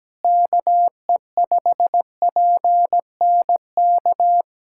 Morse code, heard as a keyed tone, about 17 wpm, 700 Hz. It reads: KE5PNK
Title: USA Callsign DE Callsign - 6 Characters - 17wpm